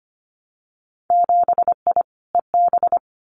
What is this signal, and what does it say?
Morse code, heard as a keyed tone, about 25 wpm, 700 Hz. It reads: 7I EB
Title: Winter Field Day Exchanges Encore 25wpm